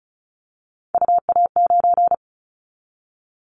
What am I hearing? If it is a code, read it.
Morse code, heard as a keyed tone, about 34 wpm, 700 Hz. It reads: UA9
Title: Callsign Prefixes for Top 100 DXCC - 35wpm